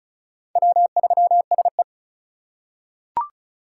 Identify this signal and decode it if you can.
Morse code, heard as a keyed tone, about 36 wpm, 700 Hz. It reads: W3SE
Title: USA Call Signs - 4x Character Spacing - 35wpm